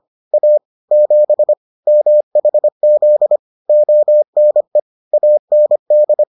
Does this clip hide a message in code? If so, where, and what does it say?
Yes: Morse code from 0.3 to 6.2 s, heard as a keyed tone, about 25 wpm, 600 Hz. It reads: A 7 MHZ ONE AND